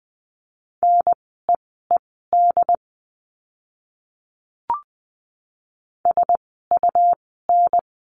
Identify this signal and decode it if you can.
Morse code, heard as a keyed tone, about 20 wpm, spaced as for 15 wpm, 700 Hz. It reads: NEED SUN
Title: 101 Lesson 17f - Letter u - Words 20wpm